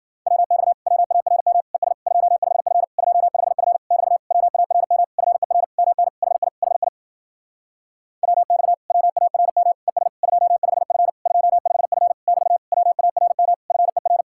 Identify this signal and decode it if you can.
Morse code, heard as a keyed tone, about 52 wpm, 700 Hz. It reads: WX WARM ES 25F 25F <BT> WARM FER MN HI HI WX WARM ES 25F 25F <BT> WARM FER